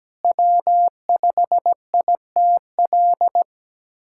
Morse code, heard as a keyed tone, about 17 wpm, 700 Hz. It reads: W5ITL
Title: USA Callsign DE Callsign - 5 Characters - 17wpm